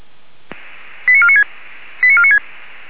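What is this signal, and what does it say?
Five-tone selective call (ZVEI), heard as quick runs of five tones, tones of about 70 ms. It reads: 89496 89496
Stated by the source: ZVEI five-tone selective call received on SDR.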